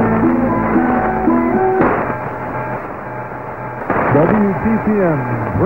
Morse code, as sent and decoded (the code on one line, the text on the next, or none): none